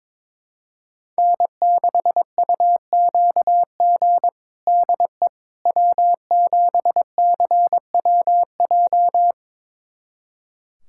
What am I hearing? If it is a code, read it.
Morse code, heard as a keyed tone, about 22 wpm, 700 Hz. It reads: N6UQG DE W7CWJ
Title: USA Callsign DE Callsign - 5 Characters - 22wpm